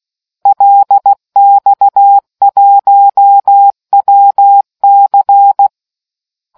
Morse code, heard as a keyed tone, about 16 wpm, 800 Hz. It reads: LX1WC